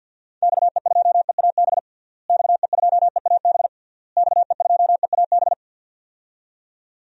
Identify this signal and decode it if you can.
Morse code, heard as a keyed tone, about 49 wpm, 700 Hz. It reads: XE2EAB XE2EAB XE2EAB
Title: DXCC-100 Call Signs - 6 Characters - Repeated 3x - 50wpm